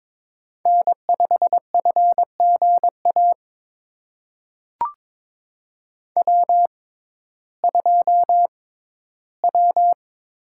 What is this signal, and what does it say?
Morse code, heard as a keyed tone, about 22 wpm, spaced as for 14 wpm, 700 Hz. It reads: N5FGA W2W
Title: USA Call Signs - 5 Characters - 6x Character Spacing - 22wpm